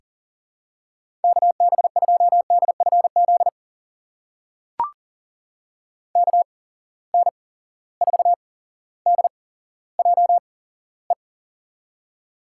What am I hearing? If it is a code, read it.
Morse code, heard as a keyed tone, about 42 wpm, spaced as for 17 wpm, 700 Hz. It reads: KB2DFZ KN4DJE
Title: USA Call Signs - 6 Characters - 8x Character Spacing - 40wpm